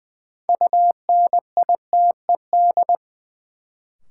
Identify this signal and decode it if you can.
Morse code, heard as a keyed tone, about 20 wpm, 700 Hz. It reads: UNITED
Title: Sentences from Top 600 Words - New Word Drill - 20wpm